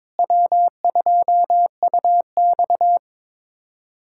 Morse code, heard as a keyed tone, about 22 wpm, 700 Hz. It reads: W2UX
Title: USA Callsign DE Callsign - 4 Characters - 22wpm